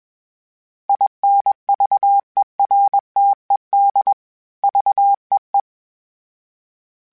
Morse code, heard as a keyed tone, about 21 wpm, 800 Hz. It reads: INVERTED VEE
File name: fromSMs / FromSMs mp3s / Antennas@20